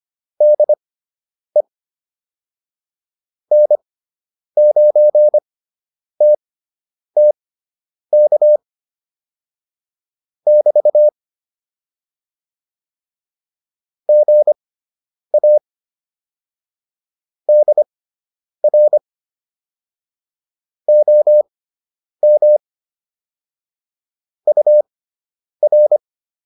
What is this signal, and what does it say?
Morse code, heard as a keyed tone, about 25 wpm, spaced as for 9 wpm, 600 Hz. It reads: DE N9TTK <BT> GA DR OM UR